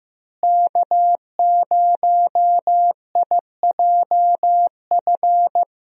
Morse code, heard as a keyed tone, about 15 wpm, 700 Hz. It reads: K0IJF
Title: USA Call Signs - 5 Characters - Repeated 3x - 15wpm